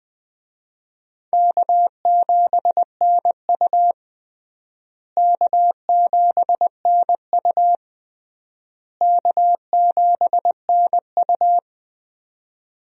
Morse code, heard as a keyed tone, about 20 wpm, 700 Hz. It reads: K7NU K7NU K7NU
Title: USA Call Signs - 4 Characters - Repeated 3x - 20wpm